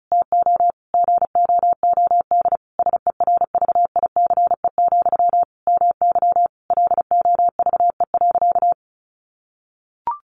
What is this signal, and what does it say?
Morse code, heard as a keyed tone, about 34 wpm, 700 Hz. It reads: TO GOOD SERVICE, MY LOVE.
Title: Sentences from Top 1100 Words 35wpm